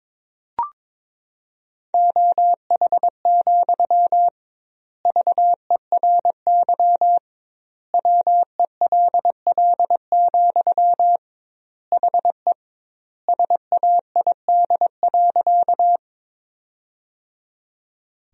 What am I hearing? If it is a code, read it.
Morse code, heard as a keyed tone, about 22 wpm, 700 Hz. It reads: OH, VERY WELL, HE SAID.
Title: 2x Word-Spacing - Sentences from Top 500 Words - 22wpm